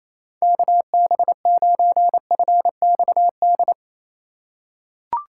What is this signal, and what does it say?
Morse code, heard as a keyed tone, about 28 wpm, 700 Hz. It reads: KB9FXD
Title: USA Call Signs - 6 Characters - 8x Character Spacing - 28wpm